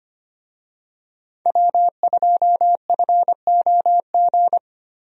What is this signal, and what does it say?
Morse code, heard as a keyed tone, about 25 wpm, 700 Hz. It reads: W2FOG
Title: USA Call Signs - 5 Characters - Repeated 3x - 25wpm